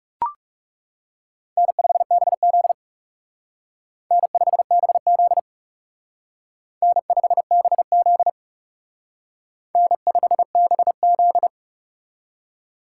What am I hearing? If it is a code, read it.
Morse code, heard as a keyed tone, about 37 wpm, 700 Hz. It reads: N5BZ N5BZ N5BZ N5BZ
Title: Speed Racing - USA 1x2 Call Signs 30wpm